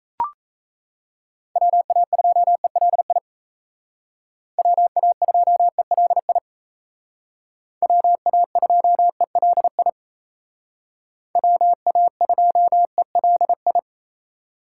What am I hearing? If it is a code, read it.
Morse code, heard as a keyed tone, about 36 wpm, 700 Hz. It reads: WA2ELI WA2ELI WA2ELI WA2ELI